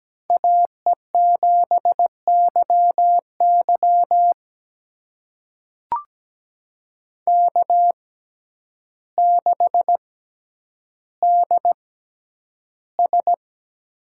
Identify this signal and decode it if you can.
Morse code, heard as a keyed tone, about 17 wpm, spaced as for 9 wpm, 700 Hz. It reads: AE7YY K6DS
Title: USA Call Signs - 6x Character Spacing - 17wpm